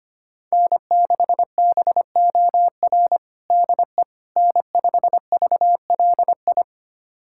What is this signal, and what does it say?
Morse code, heard as a keyed tone, about 25 wpm, 700 Hz. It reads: N6BOR DE N5VLI
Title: USA Callsign DE Callsign - 5 Characters - 25wpm